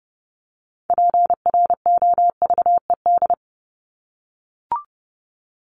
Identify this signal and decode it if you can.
Morse code, heard as a keyed tone, about 30 wpm, 700 Hz. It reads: PROVED
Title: ED Suffix - Top 100 Words with Base Words 30wpm